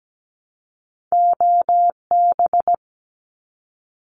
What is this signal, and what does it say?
Morse code, heard as a keyed tone, about 17 wpm, 700 Hz. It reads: OB
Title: Abbreviations 17wpm